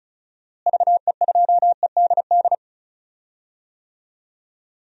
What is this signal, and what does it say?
Morse code, heard as a keyed tone, about 36 wpm, 700 Hz. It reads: VE2EDD
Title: DXCC-100 Call Signs - 6 Characters - Repeated 2x - 35wpm